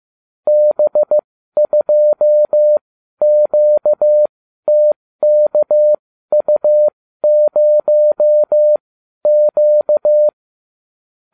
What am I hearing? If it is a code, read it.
Morse code, heard as a keyed tone, about 15 wpm, spaced as for 12 wpm, 600 Hz. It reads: B2QTKU0Q